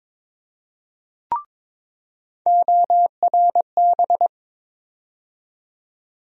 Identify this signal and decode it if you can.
Morse code, heard as a keyed tone, about 22 wpm, 700 Hz. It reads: ORB
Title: USA Call Sign Postfixes - 3 Letters 22wpm